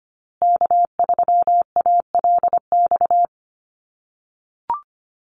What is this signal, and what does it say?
Morse code, heard as a keyed tone, about 25 wpm, 700 Hz. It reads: K3ALX
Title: USA Call Signs Repeated 3x 25wpm